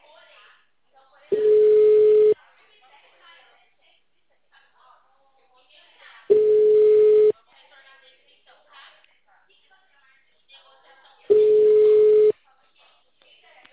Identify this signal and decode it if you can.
Ringback tone (Europe, 425 Hz), with speech.
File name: out- -2015- - - .276